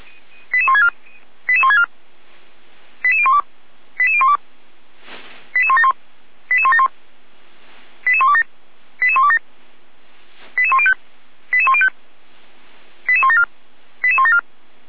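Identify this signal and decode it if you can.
Five-tone selective call (ZVEI), heard as quick runs of five tones, tones of about 70 ms. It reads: 88164 88164 88012 88012 88171 88171 88127 88127 88105 88105 88174 88174
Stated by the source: ZVEI five-tone selective call received via SDR.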